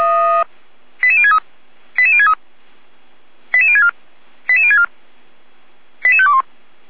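Five-tone selective call (ZVEI), heard as quick runs of five tones, tones of about 70 ms. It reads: 88063 88063 88064 88064 88521
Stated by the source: ZVEI five-tone selective call received by SDR, on the server of Feuerwehr Orlamünde (fire brigade).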